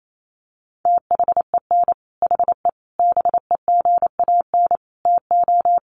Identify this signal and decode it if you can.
Morse code, heard as a keyed tone, about 28 wpm, 700 Hz. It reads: THEN HE BEGAN TO
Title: Sentences from Top 1200 Words 28wpm